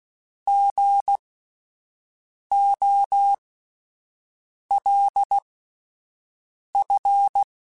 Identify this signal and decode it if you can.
Morse code, heard as a keyed tone, about 16 wpm, spaced as for 5 wpm, 800 Hz. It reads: GOLF